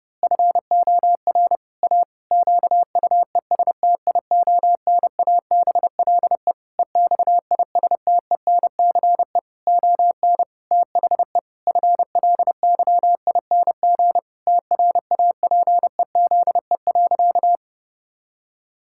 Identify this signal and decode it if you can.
Morse code, heard as a keyed tone, about 30 wpm, 700 Hz. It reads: FOR A QUESTIONABLE EXISTENCE ON THE FLYING TRAPEZE.
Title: Pangram Challenge v2 - 30wpm